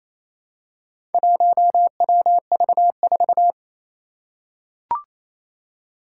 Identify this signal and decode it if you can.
Morse code, heard as a keyed tone, about 28 wpm, 700 Hz. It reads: 1WV4
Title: Four Challenging Letter-Number Combinations 28wpm